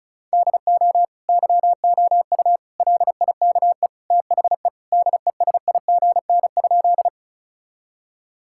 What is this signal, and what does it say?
Morse code, heard as a keyed tone, about 36 wpm, 700 Hz. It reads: DO YOU LIKE THE DESIGN?
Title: Sentences from Top 1000 Words - Review Part 5 - 35wpm